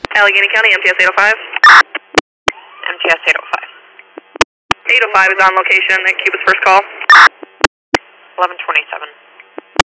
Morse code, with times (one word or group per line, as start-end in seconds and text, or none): none